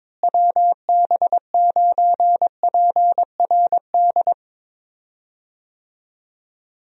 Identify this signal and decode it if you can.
Morse code, heard as a keyed tone, about 22 wpm, 700 Hz. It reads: WB9PRD
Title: USA Call Signs - 6 Characters - Repeated 2x - 22wpm